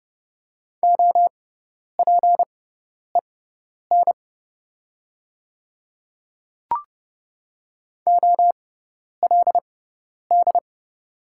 Morse code, heard as a keyed tone, about 30 wpm, spaced as for 10 wpm, 700 Hz. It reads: OPEN OLD
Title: 6x Character Spacing - Top 500 Words - 30wpm